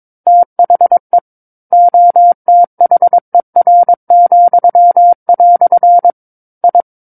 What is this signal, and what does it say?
Morse code, heard as a keyed tone, about 22 wpm, 700 Hz. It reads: THE OTHER," I
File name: Book One / 13-Chapter 13 - 22 WPM